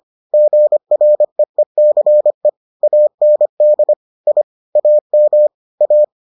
Morse code, heard as a keyed tone, about 25 wpm, 600 Hz. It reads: GREECE AND I AM A